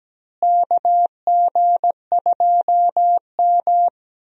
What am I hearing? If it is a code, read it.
Morse code, heard as a keyed tone, about 17 wpm, 700 Hz. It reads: KG2M
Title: USA Call Signs - 4 Characters - Repeated 2x - 17wpm